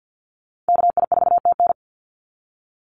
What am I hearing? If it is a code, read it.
Morse code, heard as a keyed tone, about 50 wpm, 700 Hz. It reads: KI4TN
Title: NA SOTA Activator Callsigns 50wpm